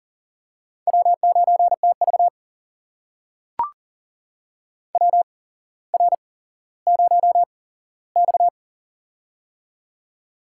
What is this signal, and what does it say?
Morse code, heard as a keyed tone, about 40 wpm, spaced as for 17 wpm, 700 Hz. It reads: W9TV WR0X
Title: USA Call Signs - 4 Characters - 8x Character Spacing - 40wpm